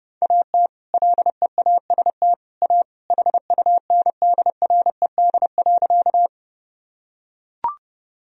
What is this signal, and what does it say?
Morse code, heard as a keyed tone, about 30 wpm, 700 Hz. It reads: AT LEAST A HUNDRED.
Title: Sentences from Top 1000 Words - Review Part 2 - 30wpm